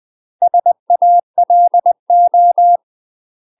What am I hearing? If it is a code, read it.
Morse code, heard as a keyed tone, about 20 wpm, 700 Hz. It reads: SALO
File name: UK MASTER QSOs TRAINING/20WPM